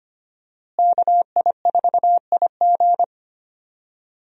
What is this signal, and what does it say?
Morse code, heard as a keyed tone, about 25 wpm, 700 Hz. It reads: KI4IG
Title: USA Call Signs - 5 Characters - Repeated 3x - 25wpm